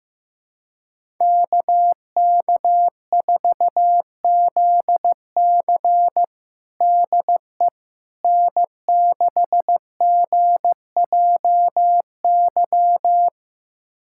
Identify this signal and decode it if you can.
Morse code, heard as a keyed tone, about 16 wpm, 700 Hz. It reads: KK4ZC DE N6GJY
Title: USA Callsign DE Callsign - 5 Characters - 15wpm